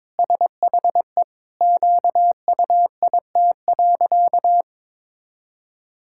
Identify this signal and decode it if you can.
Morse code, heard as a keyed tone, about 22 wpm, 700 Hz. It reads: SHE QUIT.
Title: Pangram Challenge v2 - 22wpm